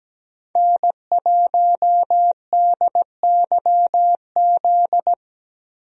Morse code, heard as a keyed tone, about 17 wpm, 700 Hz. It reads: N1DYZ
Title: USA Call Signs - 5 Characters - Repeated 3x - 17wpm